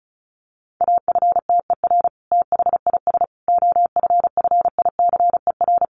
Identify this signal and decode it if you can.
Morse code, heard as a keyed tone, about 34 wpm, 700 Hz. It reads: AFTER THIS OFFICER
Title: Sentences from Top 1200 Words 35wpm